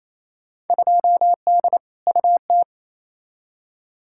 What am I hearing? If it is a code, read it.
Morse code, heard as a keyed tone, about 28 wpm, 700 Hz. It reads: 2D UT
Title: Field Day Exchanges Encore - 28wpm